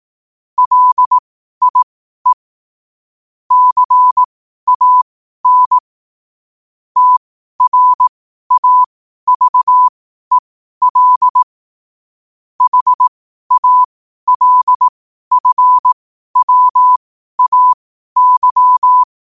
Morse code, heard as a keyed tone, about 18 wpm, spaced as for 12 wpm, 1000 Hz. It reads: LIE CAN TRAVEL HALFWAY
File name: qotdfiles / 12wpm